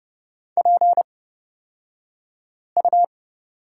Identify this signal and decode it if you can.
Morse code, heard as a keyed tone, about 30 wpm, spaced as for 5 wpm, 700 Hz. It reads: PU
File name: Lesson 38d - Letter q - Two Characters 30wpm